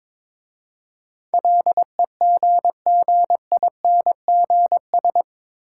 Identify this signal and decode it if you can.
Morse code, heard as a keyed tone, about 22 wpm, 700 Hz. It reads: LEGGINGS
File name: Difficult Words with Hints - 22wpm